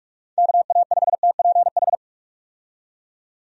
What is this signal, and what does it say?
Morse code, heard as a keyed tone, about 42 wpm, 700 Hz. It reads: KA5TPH
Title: USA Call Signs - 6 Characters - 45wpm